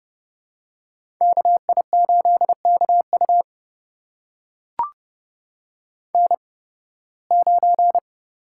Morse code, heard as a keyed tone, about 30 wpm, spaced as for 20 wpm, 700 Hz. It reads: KI8KU N9
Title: USA Call Signs - 8x Character Spacing - 30wpm